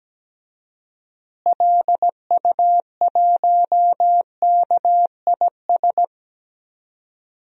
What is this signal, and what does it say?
Morse code, heard as a keyed tone, about 17 wpm, 700 Hz. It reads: LU1KIS
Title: DXCC-100 Call Signs - 6 Characters - Repeated 2x - 17wpm